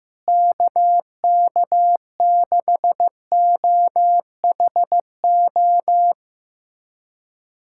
Morse code, heard as a keyed tone, about 16 wpm, 700 Hz. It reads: KK6OHO
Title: USA Callsign DE Callsign 15wpm